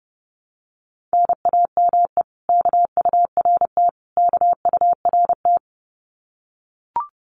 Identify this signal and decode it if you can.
Morse code, heard as a keyed tone, about 30 wpm, 700 Hz. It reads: NAME KURT KURT
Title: Common QSO Phrases Encore 30wpm